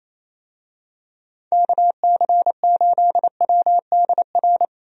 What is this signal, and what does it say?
Morse code, heard as a keyed tone, about 28 wpm, 700 Hz. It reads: KC8WDR
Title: USA Call Signs - 6 Characters - Repeated 2x - 28wpm